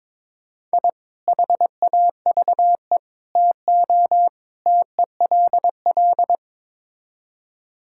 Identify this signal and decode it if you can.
Morse code, heard as a keyed tone, about 22 wpm, 700 Hz. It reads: I HAVE TO TELL
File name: practice / 22wpm / Build A Phrase - 5-Word Phrases - 22wpm